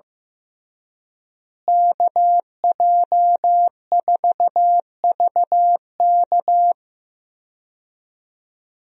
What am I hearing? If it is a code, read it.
Morse code, heard as a keyed tone, about 15 wpm, 700 Hz. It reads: KJ4VK
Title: USA Call Signs - 5 Characters - Repeated 3x - 15wpm